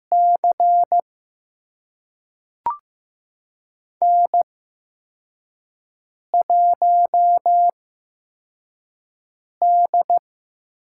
Morse code, heard as a keyed tone, about 15 wpm, spaced as for 4 wpm, 700 Hz. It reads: C N1D
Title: USA Call Signs - 5 Characters - 8x Character Spacing - 15wpm